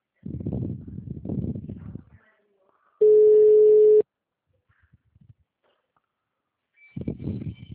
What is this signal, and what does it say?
Ringback tone (Europe, 425 Hz).